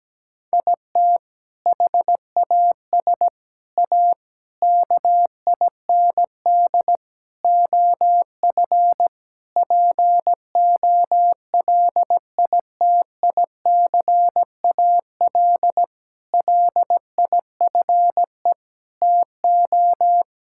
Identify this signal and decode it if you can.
Morse code, heard as a keyed tone, about 17 wpm, 700 Hz. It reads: IT HAS A KIND OF POLITICAL LIFE TO
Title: Sentences from Top 1000 Words - Review Part 5 - 17wpm